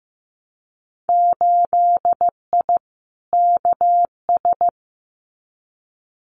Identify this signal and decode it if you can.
Morse code, heard as a keyed tone, about 15 wpm, 700 Hz. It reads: 8I KS
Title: Winter Field Day Exchanges Encore 15wpm